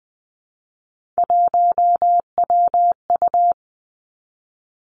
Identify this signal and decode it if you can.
Morse code, heard as a keyed tone, about 20 wpm, 700 Hz. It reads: 1WU
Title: Three Challenging Letter-Number Combinations 20wpm